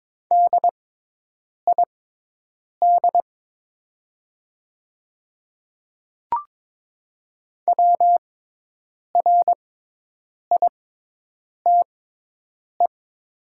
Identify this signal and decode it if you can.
Morse code, heard as a keyed tone, about 22 wpm, spaced as for 8 wpm, 700 Hz. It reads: DID WRITE
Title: 6x Character Spacing - Top 200 Words - 22wpm